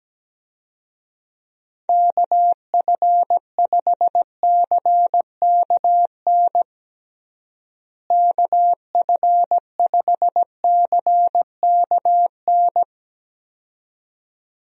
Morse code, heard as a keyed tone, about 17 wpm, 700 Hz. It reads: KF5CKN KF5CKN
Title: USA Call Signs - 6 Characters - Repeated 2x - 17wpm